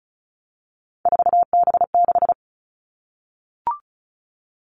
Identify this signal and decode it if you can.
Morse code, heard as a keyed tone, about 36 wpm, 700 Hz. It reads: 4B6
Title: Three Challenging Letter-Number Combinations 35wpm